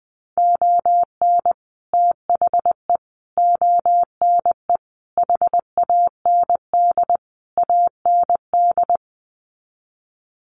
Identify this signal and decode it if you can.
Morse code, heard as a keyed tone, about 20 wpm, 700 Hz. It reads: ON THE ONE HAND AND
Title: Sets of 5 Words Encore2 20wpm